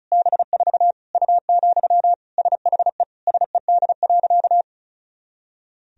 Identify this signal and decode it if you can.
Morse code, heard as a keyed tone, about 34 wpm, 700 Hz. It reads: B4 U, SHE SED.
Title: Sentences from Top 400 Words and Abbreviations 35wpm